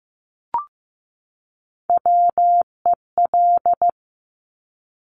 Morse code, heard as a keyed tone, about 15 wpm, 700 Hz. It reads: WEL
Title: Top Three Letter Combinations in Words 15wpm